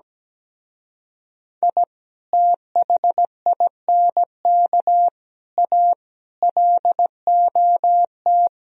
Morse code, heard as a keyed tone, about 17 wpm, 700 Hz. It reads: I THINK A LOT
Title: Speed-Racing - Sets of 5 Words 17wpm